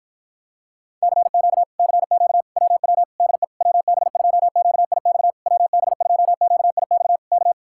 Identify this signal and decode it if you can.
Morse code, heard as a keyed tone, about 52 wpm, 700 Hz. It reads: CQ CQ WW DE WB2QIX WB2QIX K